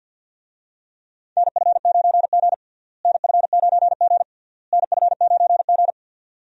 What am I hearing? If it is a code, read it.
Morse code, heard as a keyed tone, about 48 wpm, 700 Hz. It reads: NF9G NF9G NF9G
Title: USA Call Signs - 4 Characters - Repeated 3x - 50wpm